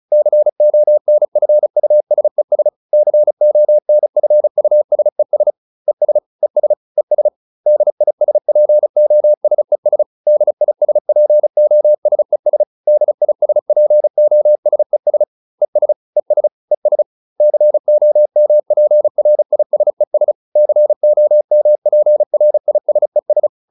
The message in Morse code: CONFUSES CONFUSES ES ES ES DISPOSES DISPOSES DISPOSES ES ES ES COMPRISES COMPRISES

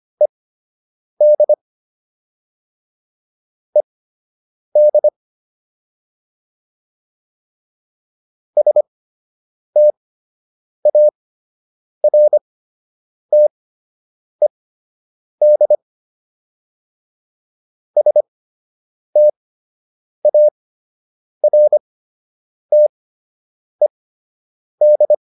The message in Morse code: ED ED STARTED STARTED

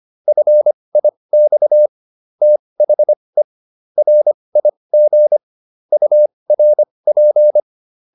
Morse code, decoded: FIX THE RIG URP

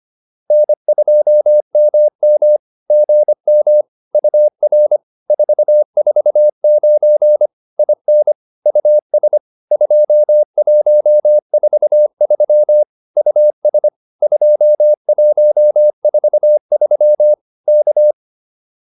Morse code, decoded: N2MM GM UR 449 IN US 2143 US 2143 K

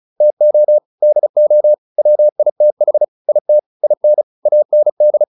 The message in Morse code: TO DO WITH IT IN AND

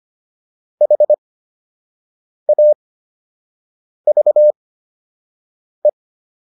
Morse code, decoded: HAVE

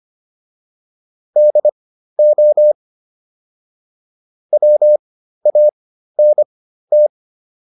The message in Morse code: DO WANT

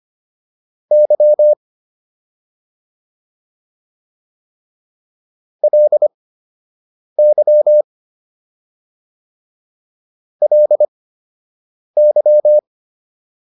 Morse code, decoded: Y LY LY

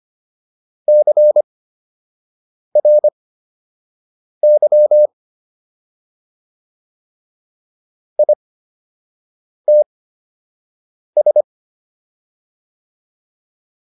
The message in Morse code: CRY ITS

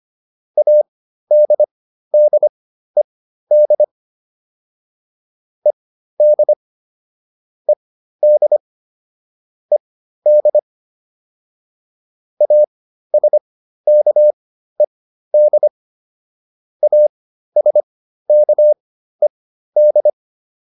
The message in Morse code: ADDED ED ED ED ASKED ASKED

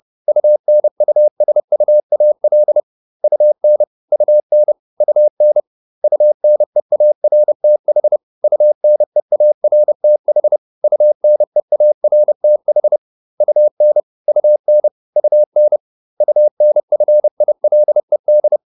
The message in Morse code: UNUSUAL UN UN UN UNEARTH UNEARTH UNEARTH UN UN UN UNFILED

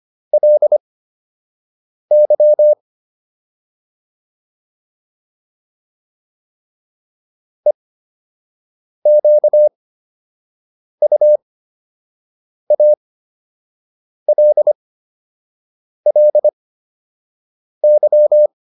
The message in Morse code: LY EQUALLY

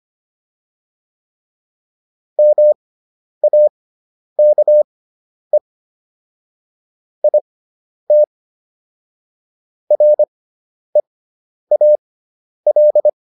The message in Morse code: MAKE IT REAL

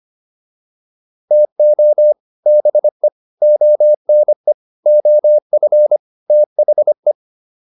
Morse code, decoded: TO BE ONE OF THE

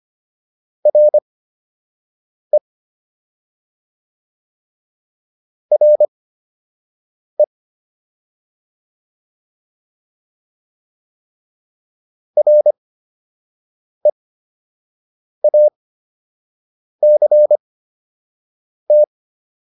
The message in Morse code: RE RE REACT

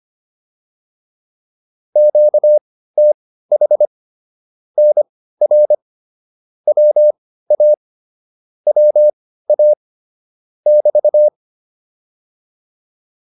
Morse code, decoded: QTH NR WA WA <BT>

